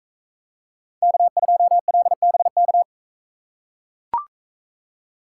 Morse code, heard as a keyed tone, about 42 wpm, 700 Hz. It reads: K2LBK